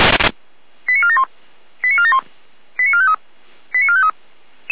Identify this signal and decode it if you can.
Five-tone selective call (ZVEI), heard as quick runs of five tones, tones of about 70 ms. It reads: 89471 89471 89453 89453